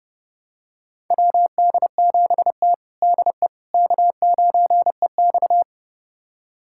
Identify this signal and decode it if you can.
Morse code, heard as a keyed tone, about 30 wpm, 700 Hz. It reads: WD7T DE K9EX
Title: USA Callsign DE Callsign - 4 Characters - 30wpm